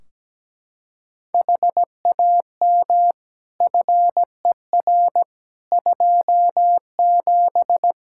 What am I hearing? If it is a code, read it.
Morse code, heard as a keyed tone, about 17 wpm, 700 Hz. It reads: HAM FER 27
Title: Practice QSO - Exchange 3 - More Info - 17wpm